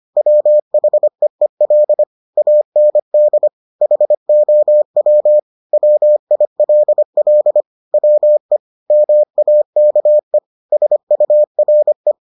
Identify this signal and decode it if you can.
Morse code, heard as a keyed tone, about 25 wpm, 600 Hz. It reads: WHEEL AND HOW WILL WE MAKE SURE